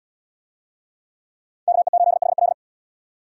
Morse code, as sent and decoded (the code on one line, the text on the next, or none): -.- . ---.. .... .-..
KE8HL